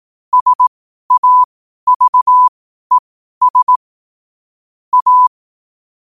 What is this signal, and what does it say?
Morse code, heard as a keyed tone, about 18 wpm, spaced as for 12 wpm, 1000 Hz. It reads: SAVES A
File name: qotdfiles / 12wpm